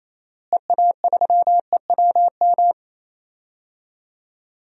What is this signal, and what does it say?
Morse code, heard as a keyed tone, about 28 wpm, 700 Hz. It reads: EA3EWM
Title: DXCC-100 Call Signs - 6 Characters - Repeated 3x - 28wpm